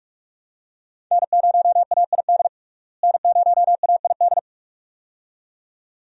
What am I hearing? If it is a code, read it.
Morse code, heard as a keyed tone, about 46 wpm, 700 Hz. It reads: N0AID N0AID
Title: USA Call Signs - 5 Characters - Repeated 2x - 45wpm